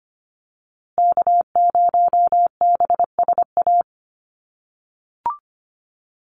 Morse code, heard as a keyed tone, about 25 wpm, 700 Hz. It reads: K0BSA
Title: USA Call Signs Repeated 3x 25wpm